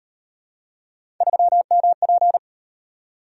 Morse code, heard as a keyed tone, about 32 wpm, 700 Hz. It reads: 3MP